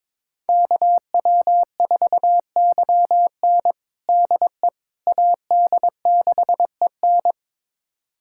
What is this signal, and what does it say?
Morse code, heard as a keyed tone, about 22 wpm, 700 Hz. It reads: KW4YN DE AD6EN
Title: USA Callsign DE Callsign - 5 Characters - 22wpm